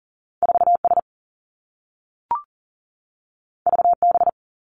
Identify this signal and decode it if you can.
Morse code, heard as a keyed tone, about 38 wpm, 700 Hz. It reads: 4S VB
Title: Two Challenging Letter-Number Combinations 40wpm